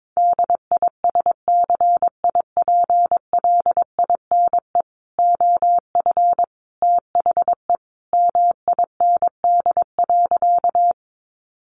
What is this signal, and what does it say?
Morse code, heard as a keyed tone, about 22 wpm, 700 Hz. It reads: DISCIPLINE OF THE MIND.
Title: Sentences from Top 900 Words 22wpm